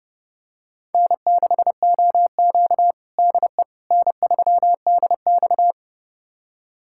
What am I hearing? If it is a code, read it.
Morse code, heard as a keyed tone, about 30 wpm, 700 Hz. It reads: N6OQ DE N3DX
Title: USA Callsign DE Callsign - 4 Characters - 30wpm